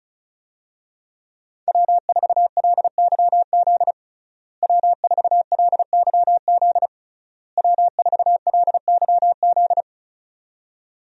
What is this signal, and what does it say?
Morse code, heard as a keyed tone, about 34 wpm, 700 Hz. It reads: W4LYZ W4LYZ W4LYZ
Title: USA Call Signs - 5 Characters - Repeated 3x - 35wpm